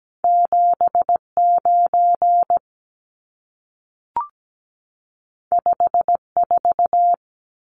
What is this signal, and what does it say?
Morse code, heard as a keyed tone, about 17 wpm, 700 Hz. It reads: 79 54
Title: Numbers - 2-Digit 17wpm